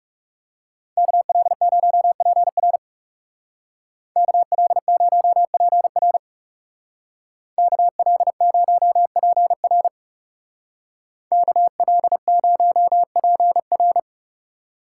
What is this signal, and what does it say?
Morse code, heard as a keyed tone, about 39 wpm, 700 Hz. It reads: KL0PR KL0PR KL0PR KL0PR